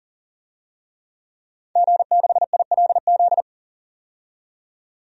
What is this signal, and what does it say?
Morse code, heard as a keyed tone, about 40 wpm, 700 Hz. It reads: G6ILZ
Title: DXCC-100 Call Signs - 5 Characters - 40wpm